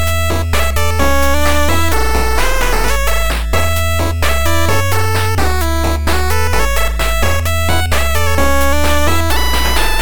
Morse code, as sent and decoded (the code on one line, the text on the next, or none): none